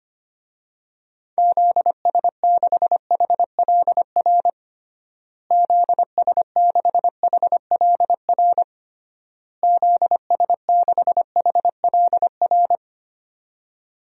Morse code, heard as a keyed tone, about 25 wpm, 700 Hz. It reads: ZS6HLR ZS6HLR ZS6HLR
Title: DXCC-100 Call Signs - 6 Characters - Repeated 3x - 25wpm